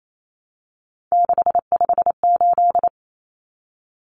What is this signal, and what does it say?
Morse code, heard as a keyed tone, about 28 wpm, 700 Hz. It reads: 658
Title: Numbers - 3-Digit 28wpm